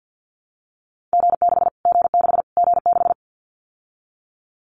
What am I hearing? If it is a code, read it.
Morse code, heard as a keyed tone, about 50 wpm, 700 Hz. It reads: G6 G6 G6
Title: Callsign Prefixes for Top 25 DXCC - Repeated 3x - 50wpm